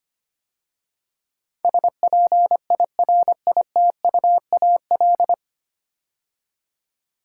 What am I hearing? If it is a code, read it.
Morse code, heard as a keyed tone, about 25 wpm, 690 Hz. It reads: SPIRITUAL
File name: practice / 25wpm / 9 Letter Words - First 3 Letters Given - 25wpm